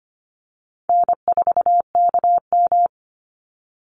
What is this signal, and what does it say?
Morse code, heard as a keyed tone, about 25 wpm, 700 Hz. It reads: N4KM
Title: Easy Call Signs Repeated 2x - 25wpm